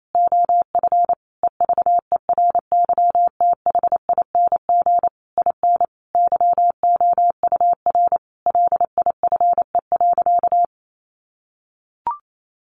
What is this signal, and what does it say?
Morse code, heard as a keyed tone, about 28 wpm, 700 Hz. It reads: OF EVERYTHING IN YOUR LIFE.
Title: Sentences from Top 700 Words 28wpm